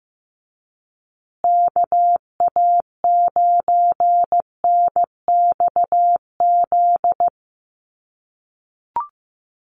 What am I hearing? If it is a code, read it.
Morse code, heard as a keyed tone, about 15 wpm, 700 Hz. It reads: KA9NXZ
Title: USA Call Signs Repeated 2x 15wpm